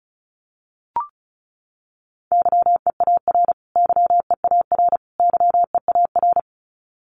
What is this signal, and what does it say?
Morse code, heard as a keyed tone, about 34 wpm, 700 Hz. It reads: YEAR YEAR YEAR
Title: Top 500 Words Repeated 35wpm